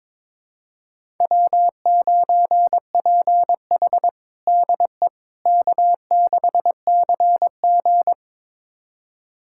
Morse code, heard as a keyed tone, about 22 wpm, 700 Hz. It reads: W9PH DE K6CG
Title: USA Callsign DE Callsign - 4 Characters - 22wpm